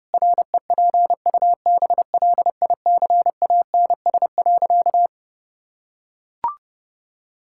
Morse code, heard as a keyed tone, about 32 wpm, 700 Hz. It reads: REPUBLICANS.
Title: Pangram Challenge v2 - 30wpm